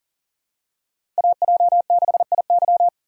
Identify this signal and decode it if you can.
Morse code, heard as a keyed tone, about 40 wpm, 700 Hz. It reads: AJ6IY
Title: USA Call Signs - 5 Characters - 2x Character Spacing - 40wpm